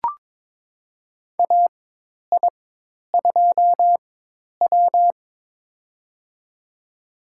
Morse code, heard as a keyed tone, about 22 wpm, spaced as for 10 wpm, 700 Hz. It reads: AI2W
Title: USA Call Signs - 4 Characters - 4x Character Spacing - 22wpm